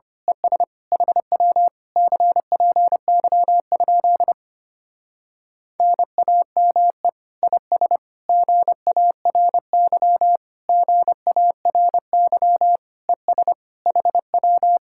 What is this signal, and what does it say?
Morse code, heard as a keyed tone, about 24 wpm, 700 Hz. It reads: ES HW CPY? NAME IS GARY GARY ES HW
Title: Speed-Racing Common QSO Phrases Encore - 25wpm